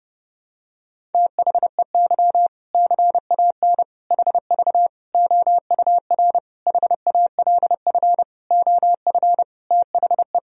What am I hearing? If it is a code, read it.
Morse code, heard as a keyed tone, about 30 wpm, 700 Hz. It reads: THEY CAN HV OUR HALF OF THE
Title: Sentences from Top 400 Words and Abbreviations 30wpm